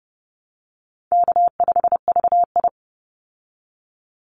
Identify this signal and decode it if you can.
Morse code, heard as a keyed tone, about 30 wpm, 700 Hz. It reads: K5VI
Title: Easy Call Signs Repeated 3x - 30wpm